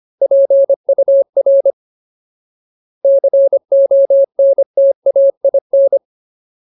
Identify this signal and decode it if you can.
Morse code, heard as a keyed tone, about 25 wpm, 550 Hz. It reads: PUR CONTAIN
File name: Ultimate Word List / 25WPM Ultimate